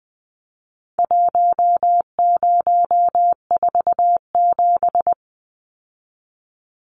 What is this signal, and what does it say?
Morse code, heard as a keyed tone, about 20 wpm, 700 Hz. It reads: 1047
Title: Numbers - Contest Style - 4-Digit 20wpm